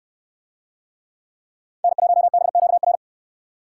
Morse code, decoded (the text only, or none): N1BQR